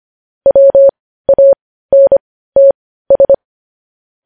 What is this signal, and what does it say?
Morse code, heard as a keyed tone, about 25 wpm, spaced as for 15 wpm, 550 Hz. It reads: WANTS
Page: news-headlines-in-morse-code-at-15-wpm